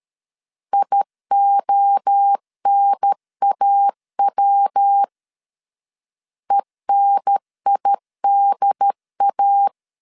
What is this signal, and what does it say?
Morse code, heard as a keyed tone, about 13 wpm, 800 Hz. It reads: IONAW ENIDA
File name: CW / DET / 1 3 1500C S C 12wpm 2750